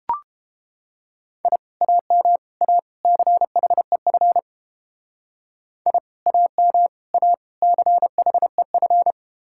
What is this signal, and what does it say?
Morse code, heard as a keyed tone, about 30 wpm, 700 Hz. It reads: I AM A CHEF I AM A CHEF